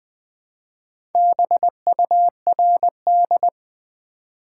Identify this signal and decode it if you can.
Morse code, heard as a keyed tone, about 20 wpm, 700 Hz. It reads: BURD
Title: Build A Word - 2 Letters At A Time - 6-Letter Words - 20wpm